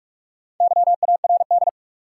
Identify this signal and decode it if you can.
Morse code, heard as a keyed tone, about 43 wpm, 700 Hz. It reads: YARD
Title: Four Letter Words - 45wpm